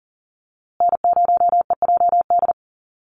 Morse code, heard as a keyed tone, about 40 wpm, 700 Hz. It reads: N0EJD
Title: USA Call Signs Repeated 2x 40wpm